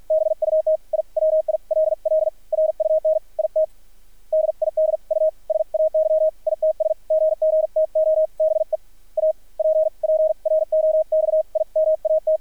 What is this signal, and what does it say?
Morse code, heard as a keyed tone, about 53 wpm, spaced as for 43 wpm, 630 Hz, with radio noise. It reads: BUT I WILL RUN IT DID U SAY ITS GG TO BE APPROXIMAT
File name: archived / CRSnet